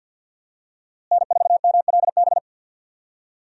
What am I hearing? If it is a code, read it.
Morse code, heard as a keyed tone, about 52 wpm, 700 Hz. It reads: N4MLB
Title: USA Call Signs - 5 Characters - Repeated 3x - 50wpm